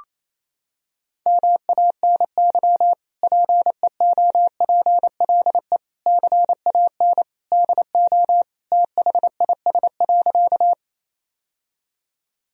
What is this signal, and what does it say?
Morse code, heard as a keyed tone, about 28 wpm, 700 Hz. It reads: MANY PEOPLE CAN DO THIS.
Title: Sentences from Top 1000 Words - Review Part 1 - 28wpm